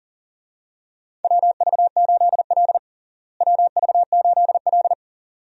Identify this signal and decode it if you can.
Morse code, heard as a keyed tone, about 40 wpm, 700 Hz. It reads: WV8L WV8L
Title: USA Call Signs - 4 Characters - Repeated 2x - 40wpm